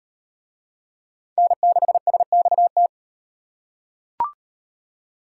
Morse code, heard as a keyed tone, about 38 wpm, 700 Hz. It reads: N6SXT